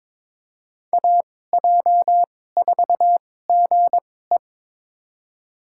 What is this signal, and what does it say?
Morse code, heard as a keyed tone, about 21 wpm, spaced as for 16 wpm, 700 Hz. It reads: AJ4GE
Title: USA Call Signs - 2x Character Spacing - 22wpm